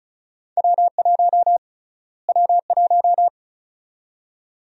Morse code, heard as a keyed tone, about 36 wpm, 700 Hz. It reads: W1 W1
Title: USA Call Sign Prefixes - Repeated 2x - 35wpm